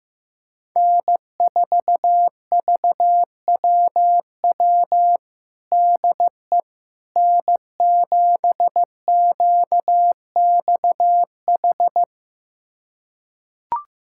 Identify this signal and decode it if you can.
Morse code, heard as a keyed tone, about 15 wpm, 700 Hz. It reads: N4VWW DE N7QXH
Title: USA Callsign DE Callsign - 5 Characters - 15wpm